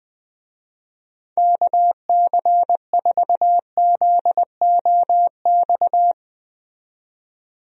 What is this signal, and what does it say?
Morse code, heard as a keyed tone, about 20 wpm, 700 Hz. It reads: KC4ZOX
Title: USA Call Signs - 6 Characters - 20wpm